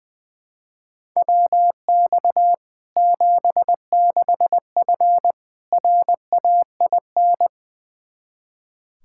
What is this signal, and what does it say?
Morse code, heard as a keyed tone, about 20 wpm, 700 Hz. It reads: WX 76F RAIN
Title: Practice QSO - Exchange 2 - Info - 20wpm